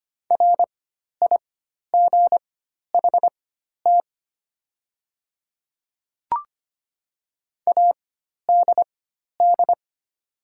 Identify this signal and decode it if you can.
Morse code, heard as a keyed tone, about 25 wpm, spaced as for 12 wpm, 690 Hz. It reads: RIGHT ADD
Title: 4x Character Spacing - Top 200 Words - 25wpm